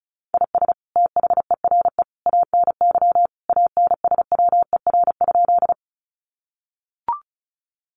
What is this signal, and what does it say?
Morse code, heard as a keyed tone, about 34 wpm, 700 Hz. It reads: IS THERE ANY ANSWER?
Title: Sentences from Top 300 Words 35wpm